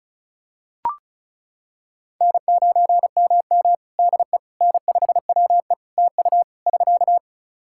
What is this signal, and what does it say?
Morse code, heard as a keyed tone, about 36 wpm, 700 Hz. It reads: N9MM DE N5WE TU <SK>
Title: Practice QSO - Exchange 4 - Ending - 35wpm